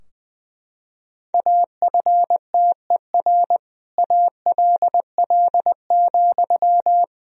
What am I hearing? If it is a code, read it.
Morse code, heard as a keyed tone, about 21 wpm, 700 Hz. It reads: AFTER ALL,
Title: Sentences from Top 1000 Words - Review Part 2 - 20wpm